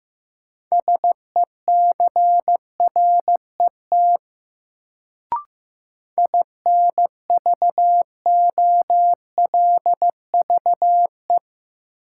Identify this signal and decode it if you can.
Morse code, heard as a keyed tone, about 15 wpm, 700 Hz. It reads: SECRET INVOLVE
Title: Sentences from Top 1200 Words - New Word Drill - 15wpm